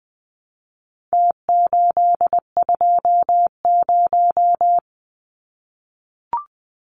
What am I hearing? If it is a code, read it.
Morse code, heard as a keyed tone, about 20 wpm, 700 Hz. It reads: T820
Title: Numbers - Contest Style - 4-Digit 20wpm